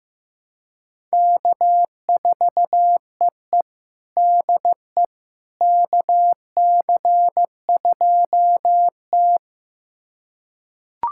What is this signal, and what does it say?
Morse code, heard as a keyed tone, about 15 wpm, 700 Hz. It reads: K4EE DE KC2T
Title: USA Callsign DE Callsign - 4 Characters - 15wpm